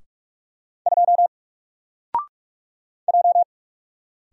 Morse code, heard as a keyed tone, about 44 wpm, 700 Hz. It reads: 2 J
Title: Single Letter-Number 45wpm